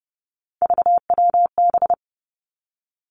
Morse code, heard as a keyed tone, about 30 wpm, 700 Hz. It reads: VWB
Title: Three Challenging Letter-Number Combinations 30wpm